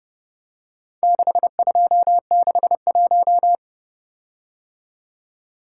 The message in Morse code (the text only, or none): 6261